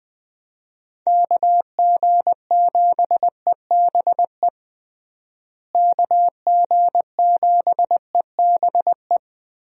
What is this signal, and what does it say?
Morse code, heard as a keyed tone, about 20 wpm, 700 Hz. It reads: KG7EBE KG7EBE
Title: USA Call Signs - 6 Characters - Repeated 3x - 20wpm